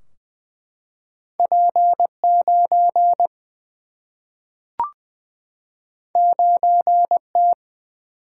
Morse code, heard as a keyed tone, about 20 wpm, 700 Hz. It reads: P9 9T
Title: Single-Letter Single-Number 20wpm